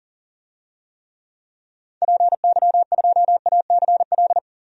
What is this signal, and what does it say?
Morse code, heard as a keyed tone, about 40 wpm, 700 Hz. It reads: PY2ACL
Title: DXCC-100 Call Signs - 6 Characters - 40wpm